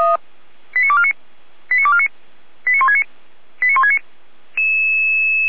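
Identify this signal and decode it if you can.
Five-tone selective call (ZVEI), heard as quick runs of five tones, tones of about 70 ms. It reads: 89249 89249 89169 89169
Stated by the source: ZVEI five-tone selective call received on an SDR.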